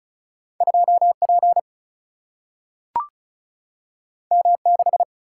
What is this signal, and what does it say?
Morse code, heard as a keyed tone, about 36 wpm, 700 Hz. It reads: 2 P M 6
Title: Single-Letter Single-Number 35wpm